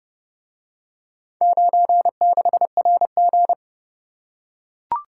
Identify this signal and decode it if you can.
Morse code, heard as a keyed tone, about 28 wpm, 700 Hz. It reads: 96RG